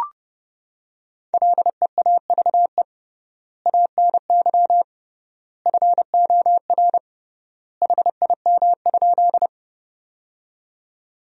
Morse code, heard as a keyed tone, about 32 wpm, 700 Hz. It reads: LEAVE ANY FOR HIM?
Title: 3x Word-Spacing - Sentences from Top 400 Words - 30wpm